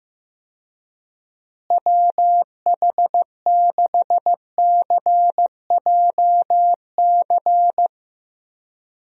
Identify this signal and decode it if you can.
Morse code, heard as a keyed tone, about 15 wpm, 700 Hz. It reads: WH6CJC
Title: USA Call Signs - 6 Characters - Repeated 3x - 15wpm